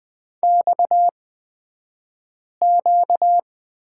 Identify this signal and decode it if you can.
Morse code, heard as a keyed tone, about 20 wpm, spaced as for 5 wpm, 700 Hz. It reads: XQ
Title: 235 Lesson 38d - Letter q - Two Characters 20wpm